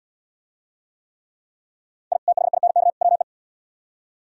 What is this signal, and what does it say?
Morse code, heard as a keyed tone, about 76 wpm, 700 Hz. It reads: I THINK WE